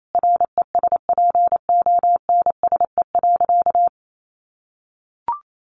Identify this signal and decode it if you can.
Morse code, heard as a keyed tone, about 28 wpm, 700 Hz. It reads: RESPONSE.
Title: Sentences from Top 1400 Words 28wpm